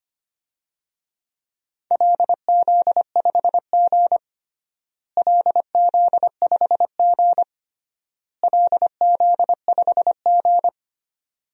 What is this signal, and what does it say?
Morse code, heard as a keyed tone, about 25 wpm, 700 Hz. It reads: LZ5G LZ5G LZ5G
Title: DXCC-100 Call Signs - 4 Characters - Repeated 3x - 25wpm